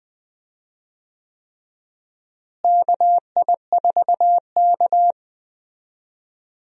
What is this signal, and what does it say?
Morse code, heard as a keyed tone, about 20 wpm, 700 Hz. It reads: KI4K
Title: USA Call Signs - 4 Characters - Repeated 3x - 20wpm